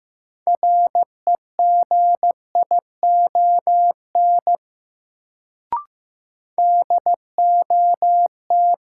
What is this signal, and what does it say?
Morse code, heard as a keyed tone, about 15 wpm, 700 Hz. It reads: REGION DOT
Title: Sentences from Top 1500 Words - New Word Drill Alt - 15wpm